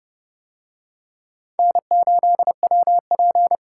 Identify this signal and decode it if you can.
Morse code, heard as a keyed tone, about 28 wpm, 700 Hz. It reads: N8WP